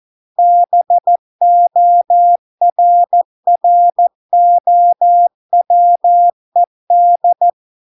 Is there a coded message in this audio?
Yes: Morse code from 0.4 to 7.5 s, heard as a keyed tone, about 14 wpm, 700 Hz. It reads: BORROWED